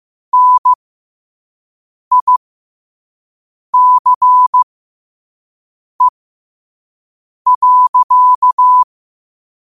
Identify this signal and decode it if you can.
Morse code, heard as a keyed tone, about 15 wpm, spaced as for 5 wpm, 1000 Hz. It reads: NICE.